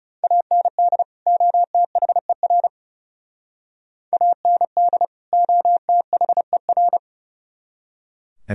Morse code, heard as a keyed tone, about 40 wpm, 700 Hz. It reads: AND OTHER AND OTHER